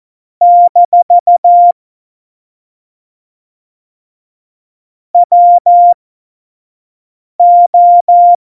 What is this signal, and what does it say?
Morse code, heard as a keyed tone, about 14 wpm, spaced as for 5 wpm, 700 Hz. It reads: - WO
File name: Morsekurs / Witze / Witz 29. September 2020, 25 F70